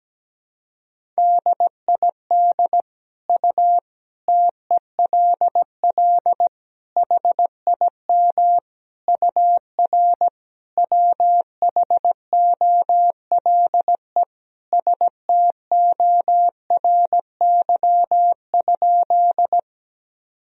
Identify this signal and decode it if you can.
Morse code, heard as a keyed tone, about 17 wpm, 700 Hz. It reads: DID U TELL HIM UR WHOLE STORY?
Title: Sentences from Top 400 Words and Abbreviations 17wpm